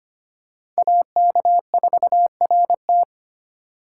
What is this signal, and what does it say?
Morse code, heard as a keyed tone, about 25 wpm, 700 Hz. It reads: AK4RT
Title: USA Call Signs - 5 Characters - 4x Character Spacing - 25wpm